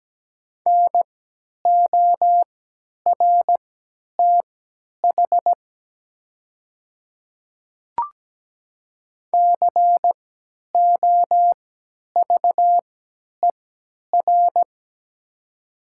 Morse code, heard as a keyed tone, about 17 wpm, spaced as for 10 wpm, 700 Hz. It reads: NORTH COVER
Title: 3x Character Spacing - Top 500 Words - 17wpm